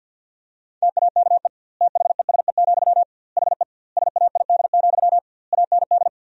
Morse code, heard as a keyed tone, about 50 wpm, 700 Hz. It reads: TAKE THESE, HE SAID, AND
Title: 2x Word-Spacing - Sentences from Top 300 Words - 50wpm